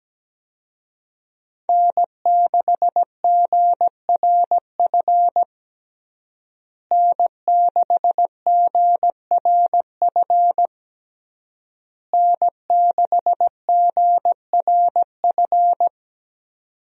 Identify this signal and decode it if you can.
Morse code, heard as a keyed tone, about 17 wpm, 700 Hz. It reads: N6GRF N6GRF N6GRF
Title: USA Call Signs - 5 Characters - Repeated 3x - 17wpm